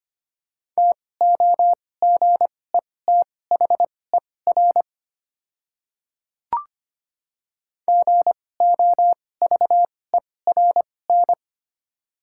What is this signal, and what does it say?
Morse code, heard as a keyed tone, about 25 wpm, spaced as for 18 wpm, 700 Hz. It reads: TOGETHER GOVERN
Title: 2x Character Spacing - Top 500 Words - 25wpm